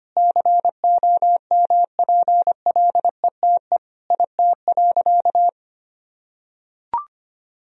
Morse code, heard as a keyed tone, about 25 wpm, 700 Hz. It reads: COMPLETE IT.
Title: Sentences from Top 1000 Words - Review Part 3 - 25wpm